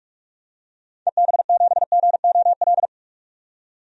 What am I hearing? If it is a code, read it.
Morse code, heard as a keyed tone, about 46 wpm, 700 Hz. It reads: EB7GOL
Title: DXCC-100 Call Signs - 6 Characters - 45wpm